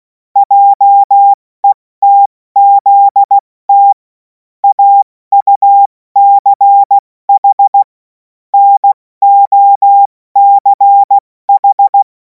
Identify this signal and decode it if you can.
Morse code, heard as a keyed tone, about 16 wpm, 800 Hz. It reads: JETZT AUCH NOCH
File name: Morsekurs / Witze / Witz 31. März 2026 70 F80